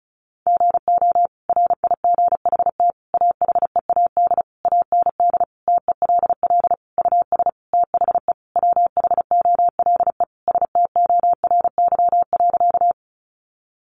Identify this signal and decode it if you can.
Morse code, heard as a keyed tone, about 34 wpm, 700 Hz. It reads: GO RIGHT AHEAD AND TELL US THE WHOLE STORY.
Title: Sentences from Top 1100 Words 35wpm